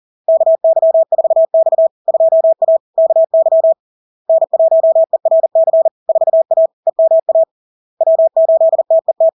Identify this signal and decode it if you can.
Morse code, heard as a keyed tone, about 40 wpm, 650 Hz. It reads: KY4X 2A KY N1ERC 4A EMA W8TET
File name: LARK-FD-2022-Log-in-CW-40wpm-650-Hz